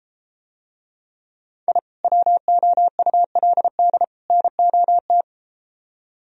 Morse code, heard as a keyed tone, about 33 wpm, 700 Hz. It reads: I WOULD NOT